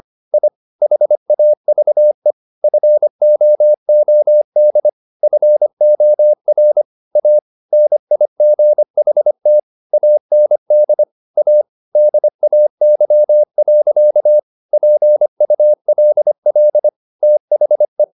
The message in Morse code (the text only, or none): I HAVE FOOD FOR A NIGHT AND A DAY. PULL THE